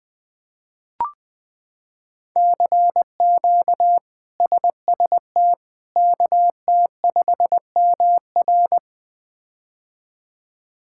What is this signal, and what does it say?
Morse code, heard as a keyed tone, about 20 wpm, 700 Hz. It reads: CQ SST KT5MR
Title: K1USN SST QSO Practice - 20wpm